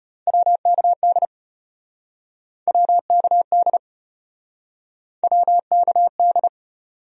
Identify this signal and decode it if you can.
Morse code, heard as a keyed tone, about 33 wpm, 700 Hz. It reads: WKD WKD WKD